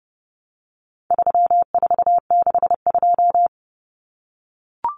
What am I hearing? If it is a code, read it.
Morse code, heard as a keyed tone, about 30 wpm, 700 Hz. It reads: 3462
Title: Common Cut Numbers - 4-Digit 30wpm